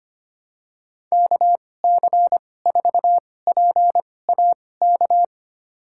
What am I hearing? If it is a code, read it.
Morse code, heard as a keyed tone, about 25 wpm, spaced as for 18 wpm, 700 Hz. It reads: KC4PAK
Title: USA Call Signs - 6 Characters - 2x Character Spacing - 25wpm